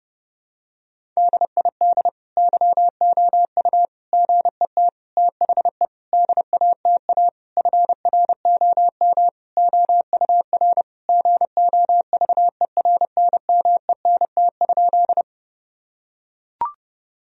Morse code, heard as a keyed tone, about 30 wpm, 700 Hz. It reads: DID YOU GET THE DATA FROM OUR GOVERNMENT?
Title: Sentences from Top 1000 Words - Review Part 3 - 30wpm